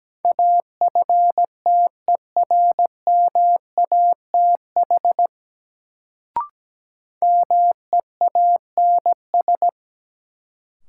Morse code, heard as a keyed tone, about 17 wpm, 700 Hz. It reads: AFTERMATH MEANS
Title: Sets of 5 Words - New Word Drill - 17wpm